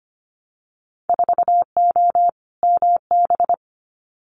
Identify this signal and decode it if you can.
Morse code, heard as a keyed tone, about 25 wpm, 700 Hz. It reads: 4O MB
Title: Winter Field Day Exchanges Encore 25wpm